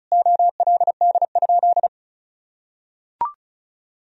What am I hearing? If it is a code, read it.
Morse code, heard as a keyed tone, about 34 wpm, 700 Hz. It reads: OLD?
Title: Sentences from Top 1000 Words - Review Part 1 - 35wpm